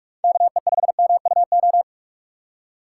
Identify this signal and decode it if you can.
Morse code, heard as a keyed tone, about 46 wpm, 700 Hz. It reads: KE5MUO
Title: USA Call Signs - 6 Characters - 4x Character Spacing - 45wpm